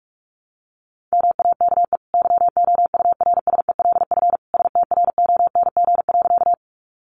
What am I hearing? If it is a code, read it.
Morse code, heard as a keyed tone, about 44 wpm, 700 Hz. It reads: MAKE YOURSELF STRONG.
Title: Sentences from Top 600 Words 45wpm